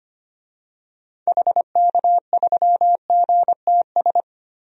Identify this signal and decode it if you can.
Morse code, heard as a keyed tone, about 25 wpm, 700 Hz. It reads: HK3GTS
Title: DXCC-100 Call Signs - 6 Characters - Repeated 3x - 25wpm